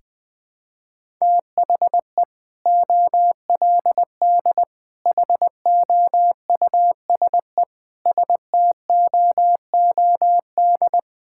Morse code, heard as a keyed tone, about 20 wpm, 700 Hz. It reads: THE OLD HOUSE STOOD
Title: Sentences with 2-Word Phrases - 20wpm